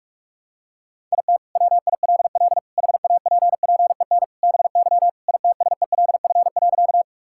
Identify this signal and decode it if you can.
Morse code, heard as a keyed tone, about 46 wpm, 700 Hz. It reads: IT WILL HAPPEN BY ITSELF.
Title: Sentences from Top 1000 Words - Review Part 5 - 45wpm